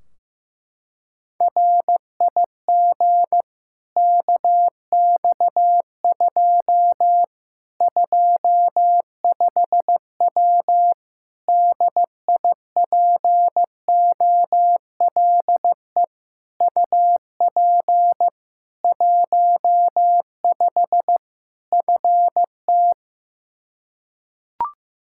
Morse code, heard as a keyed tone, about 15 wpm, 700 Hz. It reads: RIG KX2 25W DIPOLE UP 15 FT
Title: Practice QSO - Exchange 2 - Info - 15wpm